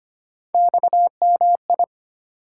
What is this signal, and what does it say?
Morse code, heard as a keyed tone, about 25 wpm, 700 Hz. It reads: XMI
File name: Three Letter-Number Combinations - 25wpm